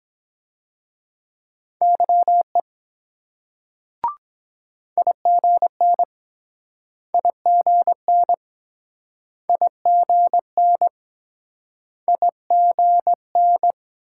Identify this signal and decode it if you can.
Morse code, heard as a keyed tone, about 24 wpm, 700 Hz. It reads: YE IGN IGN IGN IGN